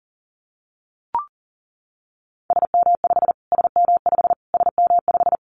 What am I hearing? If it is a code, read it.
Morse code, heard as a keyed tone, about 40 wpm, 700 Hz. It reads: SM5 SM5 SM5
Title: Callsign Prefixes for Top 25 DXCC - Repeated 3x - 40wpm